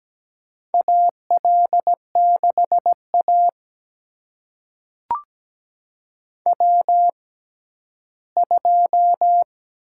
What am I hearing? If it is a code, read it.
Morse code, heard as a keyed tone, about 17 wpm, spaced as for 11 wpm, 700 Hz. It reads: AL6A W2
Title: USA Call Signs - 4 Characters - 6x Character Spacing - 17wpm